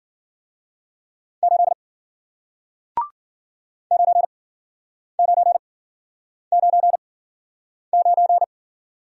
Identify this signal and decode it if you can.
Morse code, heard as a keyed tone, about 51 wpm, spaced as for 9 wpm, 700 Hz. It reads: 8 9999